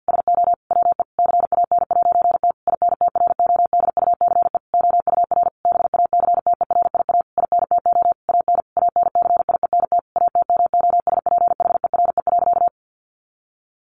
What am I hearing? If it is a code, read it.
Morse code, heard as a keyed tone, about 50 wpm, 700 Hz. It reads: SO WE CAN'T INTRODUCE OUR BACTERIA INTO AN ANCIENT ATMOSPHERE.
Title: A Sound Of Thunder 50wpm